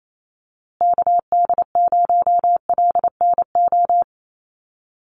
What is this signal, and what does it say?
Morse code, heard as a keyed tone, about 28 wpm, 700 Hz. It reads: KD0LNO
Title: USA Call Signs Repeated 3x 28wpm